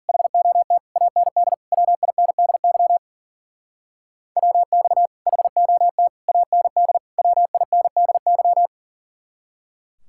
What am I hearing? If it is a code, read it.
Morse code, heard as a keyed tone, about 50 wpm, 700 Hz. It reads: HOT AND WINDY WX HOT AND WINDY
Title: Speed-Racing Common QSO Phrases - 40wpm